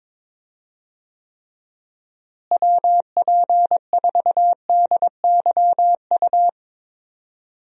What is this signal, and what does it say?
Morse code, heard as a keyed tone, about 22 wpm, 700 Hz. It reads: WP4DYU
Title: USA Call Signs - 6 Characters - Repeated 2x - 22wpm